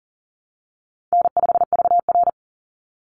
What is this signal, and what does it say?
Morse code, heard as a keyed tone, about 40 wpm, 700 Hz. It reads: N5VR
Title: Easy Call Signs Repeated 2x - 40wpm